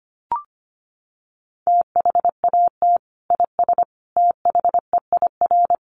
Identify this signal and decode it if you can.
Morse code, heard as a keyed tone, about 25 wpm, 700 Hz. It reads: THAT IS THEIR
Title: Sentences from Top 1400 Words 25wpm